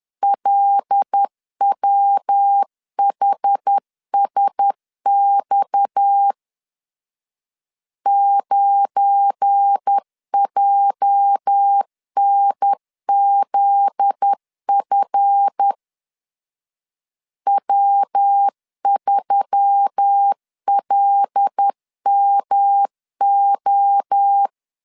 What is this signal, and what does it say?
Morse code, heard as a keyed tone, about 10 wpm, 800 Hz. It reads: LWHSX 9JNZF W3LMO
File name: LEZIONI / 8 YJ45 / 8 YJ 10wpm 2651